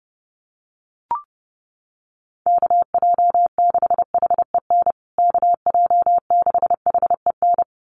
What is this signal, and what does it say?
Morse code, heard as a keyed tone, about 30 wpm, 700 Hz. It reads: KJ6HEN KJ6HEN
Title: USA Call Signs Repeated 2x 30wpm